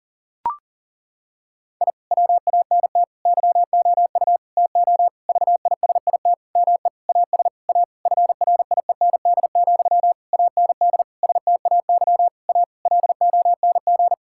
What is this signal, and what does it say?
Morse code, heard as a keyed tone, about 37 wpm, 690 Hz. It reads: I WANT YOU TO VISIT ME AS A FRIEND, AND STAY A LONG
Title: Sentences from Top 1000 Words - Review Part 6 - 40wpm